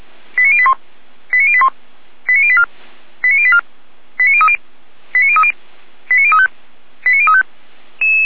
Five-tone selective call (ZVEI), heard as quick runs of five tones, tones of about 70 ms. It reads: 89081 89081 89084 89084 89030 89030 89035 89035